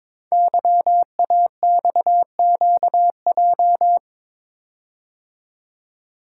Morse code, heard as a keyed tone, about 22 wpm, 700 Hz. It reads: YAXQJ
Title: Code Groups - 5 Letters - 22wpm